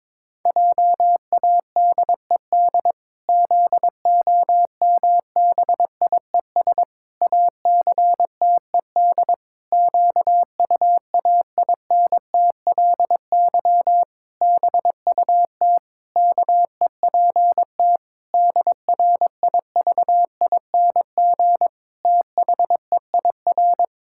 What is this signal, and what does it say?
Morse code, heard as a keyed tone, about 22 wpm, 700 Hz. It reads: JADED ZOMBIES ACTED QUAINTLY BUT KEPT DRIVING THEIR
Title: Pangram Challenge v2 - 22wpm